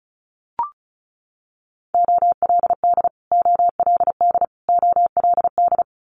Morse code, heard as a keyed tone, about 36 wpm, 700 Hz. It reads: OLD OLD OLD
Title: Multi-letter Phonemes Repeated 3x 35wpm